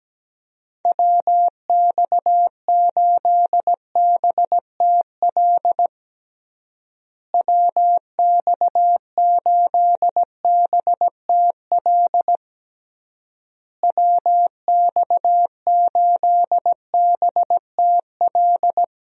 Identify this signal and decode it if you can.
Morse code, heard as a keyed tone, about 17 wpm, 700 Hz. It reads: WX8BTL WX8BTL WX8BTL
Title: USA Call Signs - 6 Characters - Repeated 3x - 17wpm